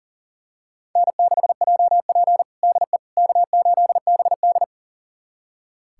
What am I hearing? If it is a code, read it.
Morse code, heard as a keyed tone, about 40 wpm, 700 Hz. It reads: N6JP DE K8BD
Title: USA Callsign DE Callsign - 4 Characters - 40wpm